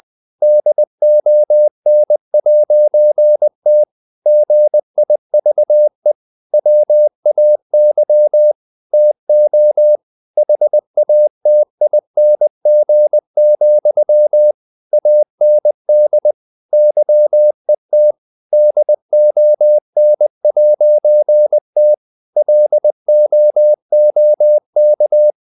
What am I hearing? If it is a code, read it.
Morse code, heard as a keyed tone, about 20 wpm, 600 Hz. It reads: DON'T GIVE WAY TO HATING, AND YET DON'T LOOK
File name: CW / Practice / If20